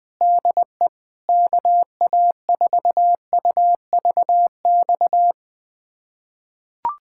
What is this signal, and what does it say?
Morse code, heard as a keyed tone, about 20 wpm, 700 Hz. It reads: DE KA4UVX
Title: USA Callsign DE Callsign - 6 Characters - 20wpm